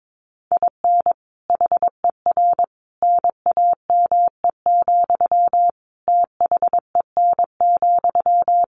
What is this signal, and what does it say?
Morse code, heard as a keyed tone, about 22 wpm, 700 Hz. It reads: IN HER NAME, THEN,
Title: Sentences from Top 500 Words 22wpm